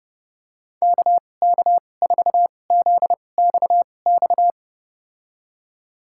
Morse code, heard as a keyed tone, about 30 wpm, spaced as for 22 wpm, 700 Hz. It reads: KK4ZXX
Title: USA Call Signs - 6 Characters - 2x Character Spacing - 30wpm